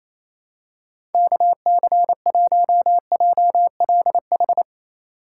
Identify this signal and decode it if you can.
Morse code, heard as a keyed tone, about 28 wpm, 700 Hz. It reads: KC1JLH
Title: USA Call Signs - 6 Characters - 2x Character Spacing - 28wpm